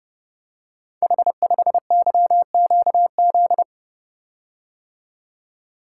Morse code, heard as a keyed tone, about 31 wpm, 700 Hz. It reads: H5YQZ
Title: Code Groups - 5 Letter:Numbers - 30wpm